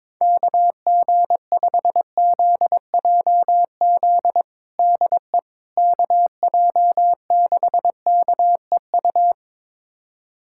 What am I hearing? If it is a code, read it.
Morse code, heard as a keyed tone, about 22 wpm, 700 Hz. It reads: KG5ZJZ DE KJ6KEU
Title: USA Callsign DE Callsign - 6 Characters - 22wpm